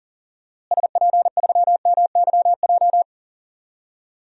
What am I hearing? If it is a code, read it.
Morse code, heard as a keyed tone, about 37 wpm, 700 Hz. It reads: SP3MYJ
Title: DXCC-100 Call Signs - 6 Characters - Repeated 3x - 40wpm